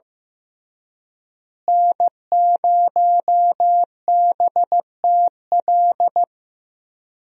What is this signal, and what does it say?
Morse code, heard as a keyed tone, about 15 wpm, 700 Hz. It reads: N0BTL
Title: USA Call Signs - 5 Characters - Repeated 3x - 15wpm